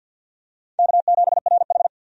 Morse code, heard as a keyed tone, about 53 wpm, 700 Hz. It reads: K7RH
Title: USA Call Signs - 4 Characters - 50wpm